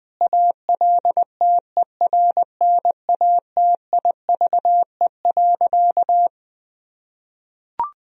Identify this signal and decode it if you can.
Morse code, heard as a keyed tone, about 20 wpm, 700 Hz. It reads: ALTERNATIVE.
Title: Sentences from Top 1000 Words - Review Part 4 - 20wpm